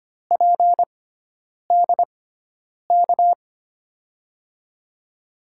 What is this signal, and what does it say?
Morse code, heard as a keyed tone, about 26 wpm, spaced as for 9 wpm, 690 Hz. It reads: PDK
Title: USA Call Signs - 5 Characters - 6x Character Spacing - 25wpm